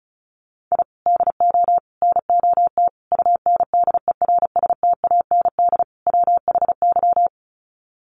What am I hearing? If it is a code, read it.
Morse code, heard as a keyed tone, about 34 wpm, 700 Hz. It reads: I DO NOT UNDERSTAND WHY
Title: Sets of 5 Words Encore 35wpm